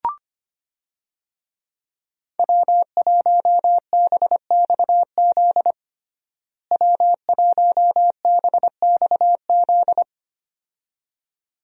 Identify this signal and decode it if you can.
Morse code, heard as a keyed tone, about 25 wpm, 700 Hz. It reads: W1BXZ W1BXZ
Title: USA Call Signs - 5 Characters - Repeated 2x - 25wpm